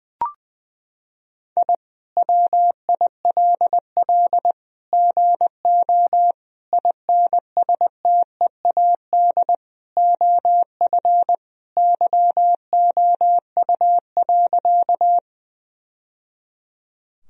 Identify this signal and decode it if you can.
Morse code, heard as a keyed tone, about 20 wpm, 700 Hz. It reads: I WILL GO INSTEAD OF YOU.
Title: Sentences from Top 1000 Words - Review Part 4 - 20wpm